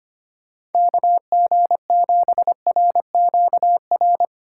Morse code, heard as a keyed tone, about 25 wpm, 700 Hz. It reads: KG7RQR
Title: USA Call Signs - 6 Characters - 3x Character Spacing - 25wpm